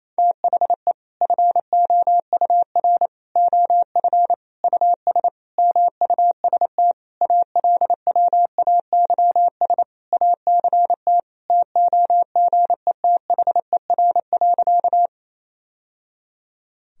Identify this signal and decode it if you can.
Morse code, heard as a keyed tone, about 27 wpm, 700 Hz. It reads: THE FOUR OF US MUST ALWAYS ACT TOGETHER.
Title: Sentences from Top 1000 Words - Review Part 1 - 28wpm